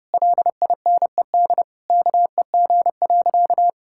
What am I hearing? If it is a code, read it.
Morse code, heard as a keyed tone, about 30 wpm, 700 Hz. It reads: LINED KEG.
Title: Pangram Challenge v2 - 30wpm